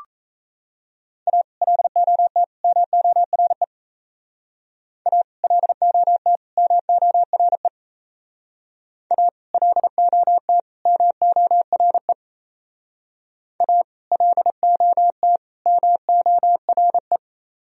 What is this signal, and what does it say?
Morse code, heard as a keyed tone, about 37 wpm, 700 Hz. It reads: A LOT MORE A LOT MORE A LOT MORE A LOT MORE